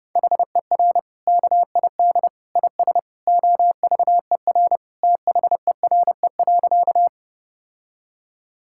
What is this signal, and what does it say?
Morse code, heard as a keyed tone, about 30 wpm, 700 Hz. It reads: HER KID IS OVER THERE.
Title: Sentences from Top 1600 Words 30wpm